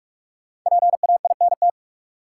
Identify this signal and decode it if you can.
Morse code, heard as a keyed tone, about 44 wpm, 700 Hz. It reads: PAINT
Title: Sentences from Top 1000 Words - Review Part 6 - New Word Drill - 45wpm